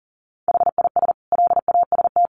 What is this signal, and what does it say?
Morse code, heard as a keyed tone, about 40 wpm, 700 Hz. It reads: HIS LAST
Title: Sentences from Top 1200 Words 40wpm